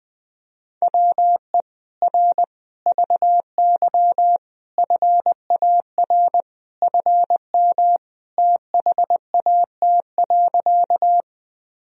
Morse code, heard as a keyed tone, about 20 wpm, 700 Hz. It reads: WE R VY FAR FM THAT.
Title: Sentences from Top 300 Words and Abbreviations 20wpm